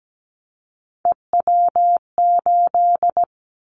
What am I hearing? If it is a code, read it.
Morse code, heard as a keyed tone, about 17 wpm, 700 Hz. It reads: EW8
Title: Callsign Prefixes for Top 50 DXCC - 17wpm